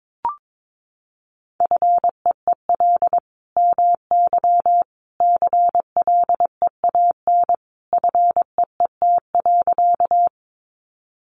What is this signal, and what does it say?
Morse code, heard as a keyed tone, about 22 wpm, 700 Hz. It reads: FEEL MY CLEAN FEET.
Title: Sentences from Top 1100 Words 22wpm